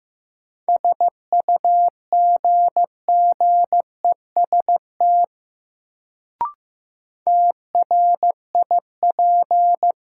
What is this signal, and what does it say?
Morse code, heard as a keyed tone, about 16 wpm, 700 Hz. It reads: SUGGEST TRIP
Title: Sentences from Top 1300 Words - New Word Drill Alt - 15wpm